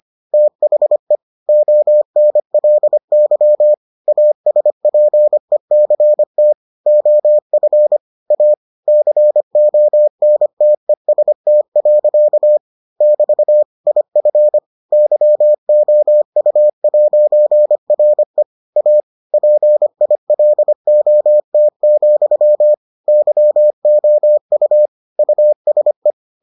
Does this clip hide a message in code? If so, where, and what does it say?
Yes: Morse code from 0.3 to 26.1 s, heard as a keyed tone, about 25 wpm, 600 Hz. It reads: THE ONLY ASPECT OF A CONTEST. <BT> IF YOU'RE A PILOT, YOU USE